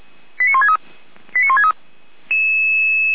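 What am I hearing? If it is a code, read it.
Five-tone selective call (ZVEI), heard as quick runs of five tones, tones of about 70 ms. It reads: 89163 89163
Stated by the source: ZVEI five-tone selective call received with an SDR, logged on the Feuerwehr Orlamünde audio server.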